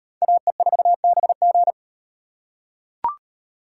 Morse code, heard as a keyed tone, about 34 wpm, 700 Hz. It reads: AE4BG